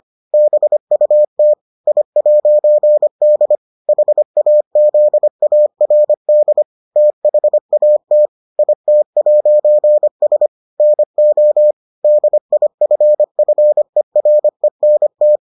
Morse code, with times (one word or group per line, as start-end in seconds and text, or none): 0.3-1.5 BUT
1.9-3.6 I'D
3.9-6.6 HAZARD
7.0-8.3 THAT
8.6-10.5 IT'S
10.8-11.7 NO
12.0-15.4 DIFFERENT